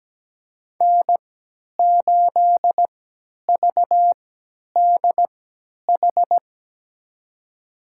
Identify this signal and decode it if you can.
Morse code, heard as a keyed tone, about 17 wpm, spaced as for 10 wpm, 700 Hz. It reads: N8VDH
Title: USA Call Signs - 5 Characters - 3x Character Spacing - 17wpm